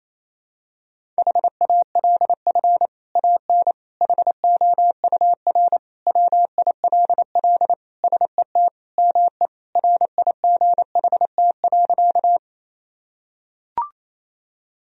Morse code, heard as a keyed tone, about 28 wpm, 700 Hz. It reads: HALF AN HOUR WILL SET ME RIGHT.
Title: Sentences from Top 1000 Words - Review Part 1 - 28wpm